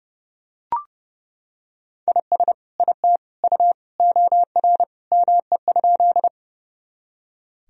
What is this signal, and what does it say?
Morse code, heard as a keyed tone, about 30 wpm, 700 Hz. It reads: IS IT U OR ME?
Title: Sentences from Top 200 Words and Abbreviations 30wpm